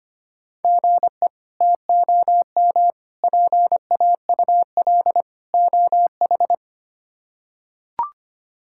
Morse code, heard as a keyed tone, about 25 wpm, 700 Hz. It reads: GE TOM PAUL OH
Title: K1USN SST QSO Practice - 25wpm